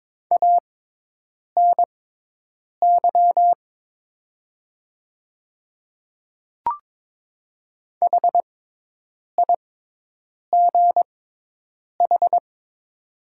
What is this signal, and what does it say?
Morse code, heard as a keyed tone, about 22 wpm, spaced as for 8 wpm, 700 Hz. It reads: ANY HIGH
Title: 6x Character Spacing - Top 200 Words - 22wpm